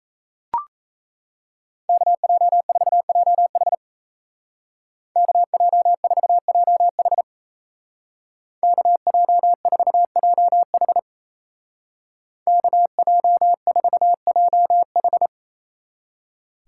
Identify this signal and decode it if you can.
Morse code, heard as a keyed tone, about 34 wpm, 700 Hz. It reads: KJ4JH KJ4JH KJ4JH KJ4JH
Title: Speed Racing - USA 2x2 Call Signs 28wpm